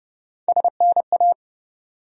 Morse code, heard as a keyed tone, about 30 wpm, 700 Hz. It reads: SNA
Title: USA Call Sign Postfixes - 3 Letters 30wpm